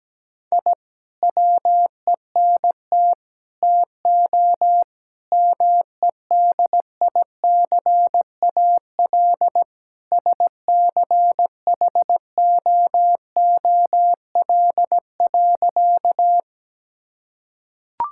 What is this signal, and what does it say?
Morse code, heard as a keyed tone, about 17 wpm, 700 Hz. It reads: I WENT TO MEDICAL SCHOOL.
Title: Sentences from Top 1000 Words - Review Part 5 - 17wpm